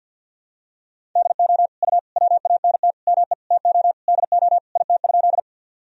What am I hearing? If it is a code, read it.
Morse code, heard as a keyed tone, about 52 wpm, 700 Hz. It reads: DO U WANT ME TO DO IT?
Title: Sentences from Top 200 Words and Abbreviations 50wpm